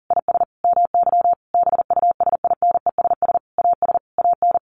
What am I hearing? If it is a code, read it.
Morse code, heard as a keyed tone, about 40 wpm, 700 Hz. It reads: IS MY BUSINESS AS AN
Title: Sentences from Top 1200 Words 40wpm